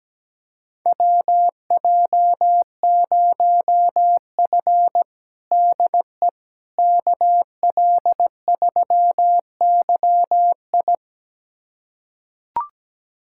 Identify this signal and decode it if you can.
Morse code, heard as a keyed tone, about 17 wpm, 700 Hz. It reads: WJ0F DE KL3YI
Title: USA Callsign DE Callsign 17wpm